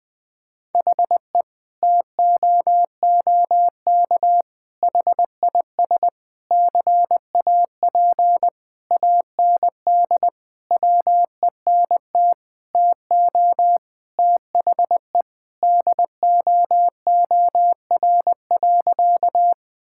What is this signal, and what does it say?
Morse code, heard as a keyed tone, about 21 wpm, 700 Hz. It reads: HE TOOK HIS CAP AND WENT TO THE DOOR.
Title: Sentences from Top 1600 Words 20wpm